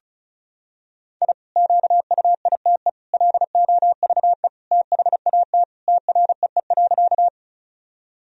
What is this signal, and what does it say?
Morse code, heard as a keyed tone, about 36 wpm, 700 Hz. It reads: I QUITE LOVE THAT TREE.
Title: Sentences from Top 1000 Words - Review Part 1 - 35wpm